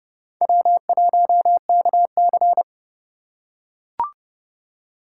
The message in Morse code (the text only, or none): W1KC